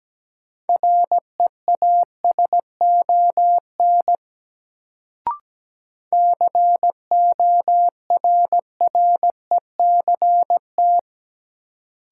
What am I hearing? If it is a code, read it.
Morse code, heard as a keyed tone, about 17 wpm, 700 Hz. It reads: REASON CORRECT
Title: Sentences from Top 700 Words - New Word Drill - 17wpm